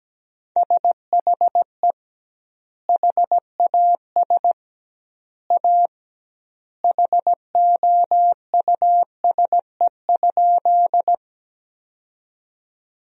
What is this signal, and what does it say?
Morse code, heard as a keyed tone, about 17 wpm, 700 Hz. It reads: SHE HAS A HOUSE?
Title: 2x Word-Spacing - Sentences from Top 200 Words - 17wpm